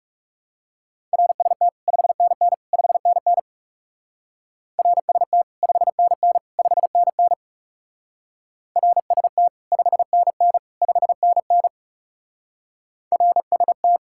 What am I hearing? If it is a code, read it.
Morse code, heard as a keyed tone, about 39 wpm, 700 Hz. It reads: RST 5NN 5NN RST 5NN 5NN RST 5NN 5NN RST